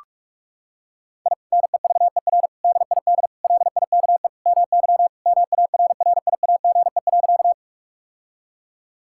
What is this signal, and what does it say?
Morse code, heard as a keyed tone, about 41 wpm, 700 Hz. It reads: I NEVER DID LIKE MY MARRIAGE.
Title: Sentences from Top 1000 Words - Review Part 5 - 45wpm